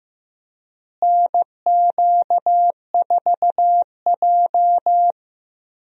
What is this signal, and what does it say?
Morse code, heard as a keyed tone, about 15 wpm, 700 Hz. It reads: NQ4J
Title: USA Call Signs - 4 Characters - Repeated 3x - 15wpm